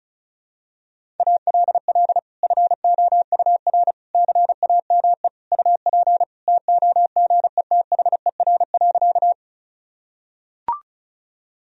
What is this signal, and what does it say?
Morse code, heard as a keyed tone, about 36 wpm, 700 Hz. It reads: ALL FOUR CAME UP TOGETHER.
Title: Sentences from Top 1000 Words - Review Part 1 - 35wpm